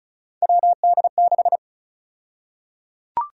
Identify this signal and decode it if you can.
Morse code, heard as a keyed tone, about 34 wpm, 700 Hz. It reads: WD6
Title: USA Call Sign Prefixes - Repeated 2x - 35wpm